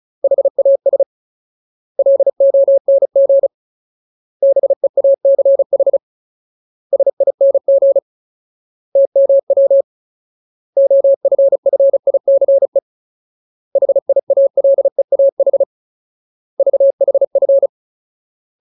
Morse code, heard as a keyed tone, about 34 wpm, 550 Hz. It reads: HAS LONG BEACH SING TMW OFFICE HIALEAH VHF
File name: Ultimate Word List / 35WPM Ultimate